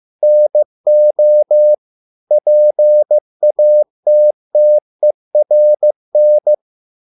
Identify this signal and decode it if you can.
Morse code, heard as a keyed tone, about 15 wpm, 600 Hz. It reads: NO PATTERN